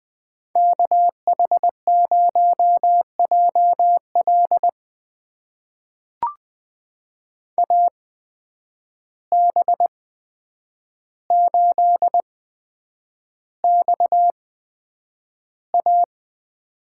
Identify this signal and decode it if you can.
Morse code, heard as a keyed tone, about 20 wpm, spaced as for 9 wpm, 700 Hz. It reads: KH0JL AB8XA
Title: USA Call Signs - 8x Character Spacing - 20wpm